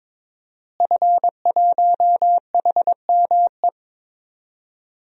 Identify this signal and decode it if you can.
Morse code, heard as a keyed tone, about 21 wpm, 700 Hz. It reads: F1HME
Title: DXCC-100 Call Signs - 5 Characters - Repeated 3x - 22wpm